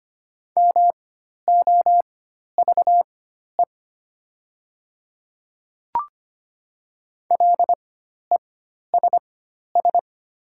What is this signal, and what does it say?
Morse code, heard as a keyed tone, about 25 wpm, spaced as for 12 wpm, 700 Hz. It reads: MOVE LESS
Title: 4x Character Spacing - Top 500 Words - 25wpm